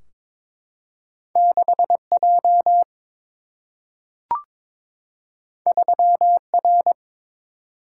Morse code, heard as a keyed tone, about 22 wpm, 700 Hz. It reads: 6J 3R
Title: Single-Letter Single-Number 22wpm